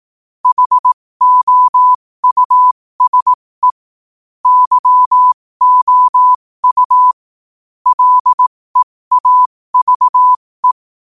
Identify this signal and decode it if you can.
Morse code, heard as a keyed tone, about 18 wpm, spaced as for 15 wpm, 1000 Hz. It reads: HOUSE YOU LEAVE
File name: qotdfiles / 15wpm